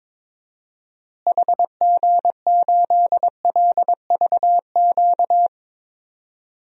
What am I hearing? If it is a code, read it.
Morse code, heard as a keyed tone, about 22 wpm, 700 Hz. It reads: HG8LVQ
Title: DXCC-100 Call Signs - 6 Characters - Repeated 3x - 22wpm